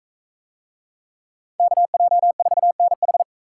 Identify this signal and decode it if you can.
Morse code, heard as a keyed tone, about 44 wpm, 700 Hz. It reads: KJ4NH